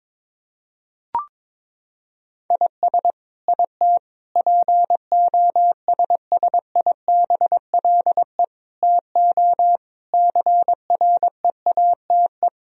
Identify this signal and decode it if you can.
Morse code, heard as a keyed tone, about 22 wpm, 700 Hz. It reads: IS IT POSSIBLE TO CREATE
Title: Sentences from Top 1000 Words - Review Part 3 - 22wpm